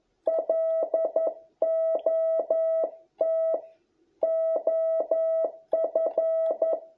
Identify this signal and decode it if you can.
Morse code, heard as a keyed tone, about 11 wpm, 650 Hz. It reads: LOT OF